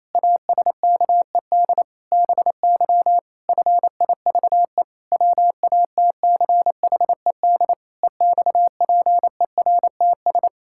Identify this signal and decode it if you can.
Morse code, heard as a keyed tone, about 28 wpm, 700 Hz. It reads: ASKED BY FIVE WATCHED EXPERTS
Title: Pangram Challenge v2 - 28wpm